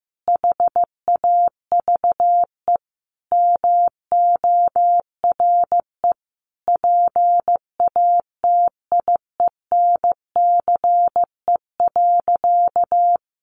Morse code, heard as a keyed tone, about 15 wpm, 700 Hz. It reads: HAVE MORE PATIENCE.
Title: Sentences from Top 1300 Words 15wpm